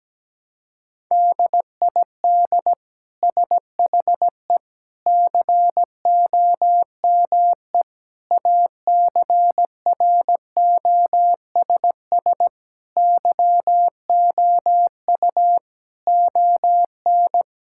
Morse code, heard as a keyed tone, about 17 wpm, 700 Hz. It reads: DID SHE COME ACROSS YOU ON
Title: Sentences from Top 1000 Words - Review Part 5 - 17wpm